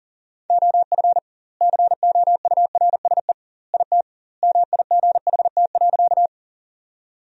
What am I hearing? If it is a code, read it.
Morse code, heard as a keyed tone, about 40 wpm, 700 Hz. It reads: OF COURSE IT MIGHT.
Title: 2x Word-Spacing - Sentences from Top 500 Words - 40wpm